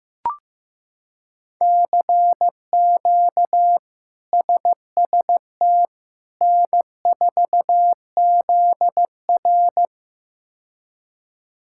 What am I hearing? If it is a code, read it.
Morse code, heard as a keyed tone, about 15 wpm, 700 Hz. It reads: CQ SST N4ZR
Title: K1USN SST QSO Practice - 15wpm